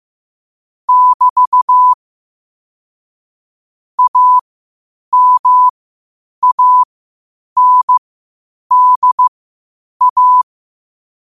Morse code, heard as a keyed tone, about 15 wpm, spaced as for 8 wpm, 1000 Hz. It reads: <BT> AMANDA
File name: qotdfiles / 8wpm